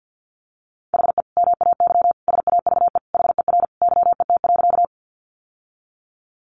Morse code, heard as a keyed tone, about 50 wpm, 700 Hz. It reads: HE MAY SAVE HER YET.
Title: Sentences from Top 1100 Words 50wpm